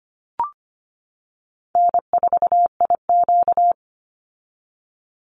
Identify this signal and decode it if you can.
Morse code, heard as a keyed tone, about 25 wpm, 700 Hz. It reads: N4IQ
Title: Call Signs - 25wpm